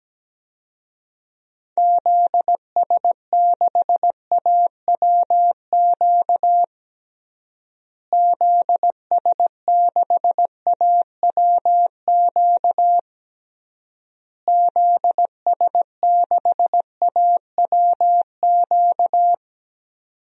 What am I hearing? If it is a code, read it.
Morse code, heard as a keyed tone, about 17 wpm, 700 Hz. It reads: ZS6AWQ ZS6AWQ ZS6AWQ
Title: DXCC-100 Call Signs - 6 Characters - Repeated 3x - 17wpm